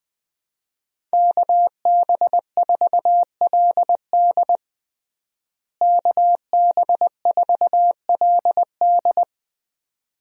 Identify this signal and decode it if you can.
Morse code, heard as a keyed tone, about 20 wpm, 700 Hz. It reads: KB4LD KB4LD
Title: USA Call Signs - 5 Characters - Repeated 2x - 20wpm